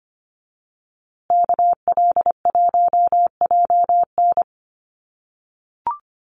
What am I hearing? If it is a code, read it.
Morse code, heard as a keyed tone, about 25 wpm, 700 Hz. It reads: KL1JN
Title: USA Call Signs Repeated 3x 25wpm